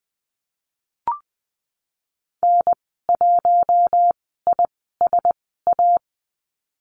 Morse code, heard as a keyed tone, about 20 wpm, spaced as for 15 wpm, 700 Hz. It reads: N1ISA
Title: 062 Lesson 11g - Letter h - Call Signs 20wpm